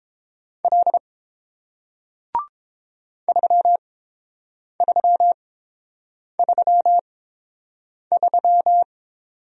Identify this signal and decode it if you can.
Morse code, heard as a keyed tone, about 30 wpm, spaced as for 7 wpm, 700 Hz. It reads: L 3333